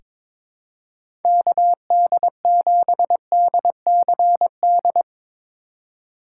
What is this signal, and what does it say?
Morse code, heard as a keyed tone, about 21 wpm, 700 Hz. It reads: KD7DCD
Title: USA Call Signs - 6 Characters - 3x Character Spacing - 22wpm